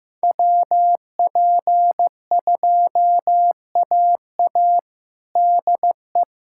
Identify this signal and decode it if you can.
Morse code, heard as a keyed tone, about 15 wpm, 700 Hz. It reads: WP2AA DE
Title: USA Callsign DE Callsign - 5 Characters - 15wpm